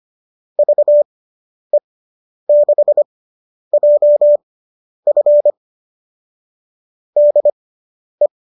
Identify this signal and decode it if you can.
Morse code, heard as a keyed tone, about 25 wpm, spaced as for 10 wpm, 590 Hz. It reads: VE6JF DE